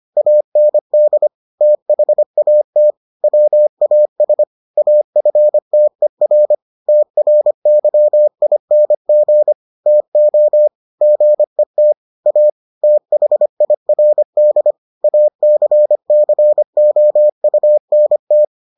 Morse code, heard as a keyed tone, about 25 wpm, 600 Hz. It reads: AND THAT WAS AFTER TRYING TO GET A THIRD ACCOUNT